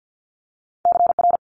Morse code, heard as a keyed tone, about 50 wpm, 700 Hz. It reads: CR
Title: Multi-letter Phonemes Quick Fire 50wpm